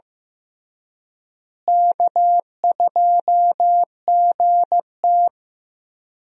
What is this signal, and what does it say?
Morse code, heard as a keyed tone, about 15 wpm, 700 Hz. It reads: K2GT
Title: USA Call Signs - 4 Characters - Repeated 3x - 15wpm